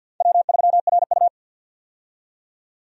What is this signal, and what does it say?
Morse code, heard as a keyed tone, about 52 wpm, 700 Hz. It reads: W3RU
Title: USA Call Signs - 4 Characters - 50wpm